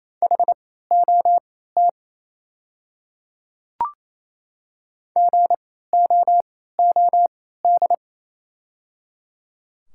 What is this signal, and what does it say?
Morse code, heard as a keyed tone, about 27 wpm, spaced as for 16 wpm, 690 Hz. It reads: HOT GOOD
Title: 3x Character Spacing - Top 500 Words - 28wpm